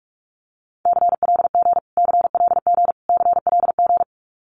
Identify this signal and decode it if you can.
Morse code, heard as a keyed tone, about 44 wpm, 700 Hz. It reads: CLG CLG CLG
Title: Abbreviations Repeated 3x - 45wpm